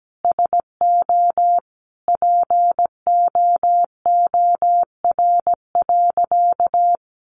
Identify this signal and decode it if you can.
Morse code, heard as a keyed tone, about 17 wpm, 700 Hz. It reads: SO POOR.
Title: Sentences from Top 1000 Words 17wpm